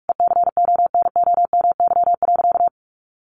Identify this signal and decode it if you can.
Morse code, heard as a keyed tone, about 44 wpm, 700 Hz. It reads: ECONOMY.
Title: Sentences from Top 1200 Words 45wpm